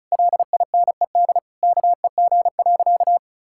Morse code, heard as a keyed tone, about 34 wpm, 700 Hz. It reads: LINED KEG.
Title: Pangram Challenge v2 - 35wpm